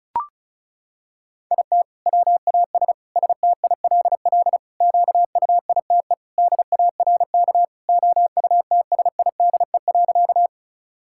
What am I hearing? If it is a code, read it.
Morse code, heard as a keyed tone, about 34 wpm, 700 Hz. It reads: IT WAS STILL QUITE DARK OUTSIDE.
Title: Sentences from Top 1000 Words - Review Part 6 - 35wpm